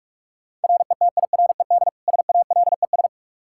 Morse code, heard as a keyed tone, about 42 wpm, 700 Hz. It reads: RETIRED SALES
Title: Practice QSO - Exchange 3 - More Info - 45wpm